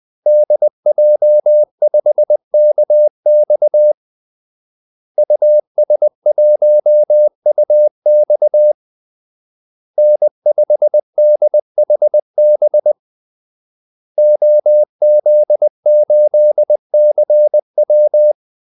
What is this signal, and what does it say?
Morse code, heard as a keyed tone, about 20 wpm, 600 Hz. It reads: DJ5KX US1UX N5DHB OZ8CW
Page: Kaldesignaler 20 wpm